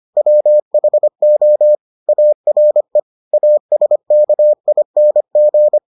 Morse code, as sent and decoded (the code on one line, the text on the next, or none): .-- .... --- / .- .-. / .- ... -.- .. -. --.
WHO ARE ASKING